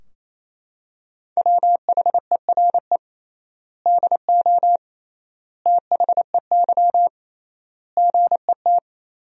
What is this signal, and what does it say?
Morse code, heard as a keyed tone, about 28 wpm, 700 Hz. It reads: WHERE DO THEY GET
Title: 3x Word-Spacing - Sentences from Top 200 Words - 28wpm